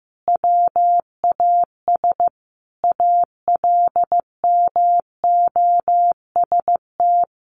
Morse code, heard as a keyed tone, about 14 wpm, 700 Hz. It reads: WAS ALMOST
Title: Sentences from Top 1100 Words 15wpm